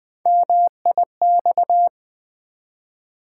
Morse code, heard as a keyed tone, about 20 wpm, 700 Hz. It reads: MIX
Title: Build A Word - 1 Letter At A Time - 3-Letter Words - 20wpm